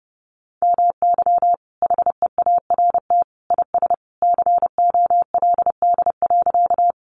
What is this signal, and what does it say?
Morse code, heard as a keyed tone, about 30 wpm, 700 Hz. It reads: MY HEART IS COLD.
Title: Sentences from Top 1200 Words 30wpm